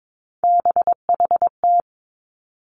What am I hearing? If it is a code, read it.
Morse code, heard as a keyed tone, about 22 wpm, 700 Hz. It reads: BHT
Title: Three Challenging Letter-Number Combinations 22wpm